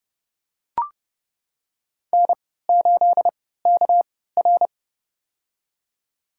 Morse code, heard as a keyed tone, about 30 wpm, spaced as for 17 wpm, 700 Hz. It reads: N8KR
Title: USA Call Signs - 4 Characters - 3x Character Spacing - 30wpm